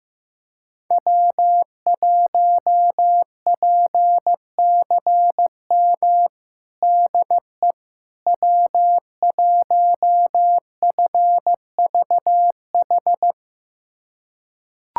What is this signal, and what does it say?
Morse code, heard as a keyed tone, about 15 wpm, 700 Hz. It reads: W1PCM DE W1FVH
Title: USA Callsign DE Callsign - 5 Characters - 15wpm